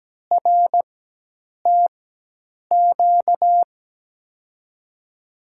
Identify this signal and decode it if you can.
Morse code, heard as a keyed tone, about 17 wpm, spaced as for 8 wpm, 700 Hz. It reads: RTQ
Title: USA Call Signs - 6 Characters - 4x Character Spacing - 17wpm